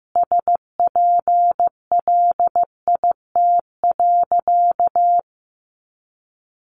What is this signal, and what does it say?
Morse code, heard as a keyed tone, about 15 wpm, 700 Hz. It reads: SPLIT.
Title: Additional Easy Phrases 15wpm